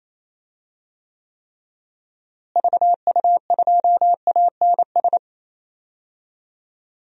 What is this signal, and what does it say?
Morse code, heard as a keyed tone, about 28 wpm, 700 Hz. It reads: VU2ANS
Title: DXCC-100 Call Signs - 6 Characters - Repeated 3x - 28wpm